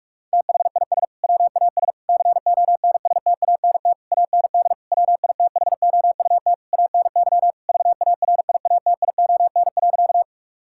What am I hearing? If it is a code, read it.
Morse code, heard as a keyed tone, about 46 wpm, 700 Hz. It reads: THIS WAS CONSTANT AND WITHOUT ANY VARIATION.
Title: Sentences from Top 1600 Words 45wpm